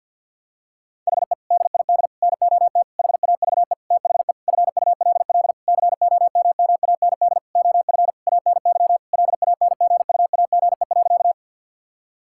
Morse code, heard as a keyed tone, about 51 wpm, 700 Hz. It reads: HE DID NOT HAVE THE FULL COMMAND OF ANY LANGUAGE.
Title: Sentences from Top 1600 Words 50wpm